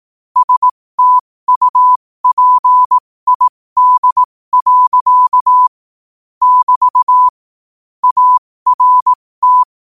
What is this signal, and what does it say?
Morse code, heard as a keyed tone, about 18 wpm, spaced as for 15 wpm, 1000 Hz. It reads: STUPID. <BT> ART
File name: 15wpm